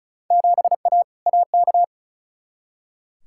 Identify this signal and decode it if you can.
Morse code, heard as a keyed tone, about 36 wpm, 700 Hz. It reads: 7A AK
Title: Field Day Exchanges Encore - 35wpm